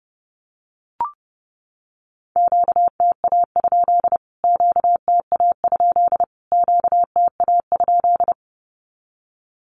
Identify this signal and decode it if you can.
Morse code, heard as a keyed tone, about 30 wpm, 700 Hz. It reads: QTA? QTA? QTA?
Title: Q-Codes Repeated 3x 30wpm